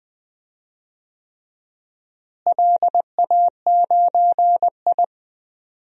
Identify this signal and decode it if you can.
Morse code, heard as a keyed tone, about 21 wpm, 700 Hz. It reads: LA9I
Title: DXCC-100 Call Signs - 4 Characters - Repeated 3x - 20wpm